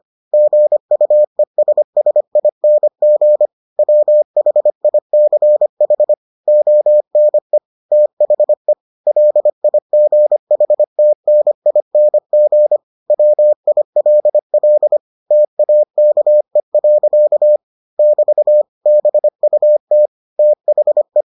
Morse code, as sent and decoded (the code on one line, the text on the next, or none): --. ..- . ... ... .. -. --. / .-- .... .. -.-. .... / --- -. / - .... / .-.. .. --. .... - -. .. -. --. / .-- .. .-.. .-.. / - .- -.- . .-.-.- / -...- / -... ..- - / - .... .
GUESSING WHICH ONE THE LIGHTNING WILL TAKE. <BT> BUT THE